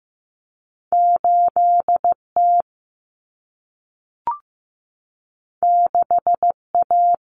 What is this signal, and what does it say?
Morse code, heard as a keyed tone, about 15 wpm, 700 Hz. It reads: 8T 6A
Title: Common Cut Numbers - 2-Digit 15wpm